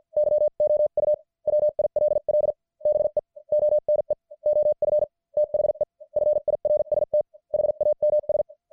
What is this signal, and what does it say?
Morse code, heard as a keyed tone, about 50 wpm, spaced as for 44 wpm, 600 Hz. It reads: YOU WILL BE ONE OF THE FIRST HAMS
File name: archived / CWBros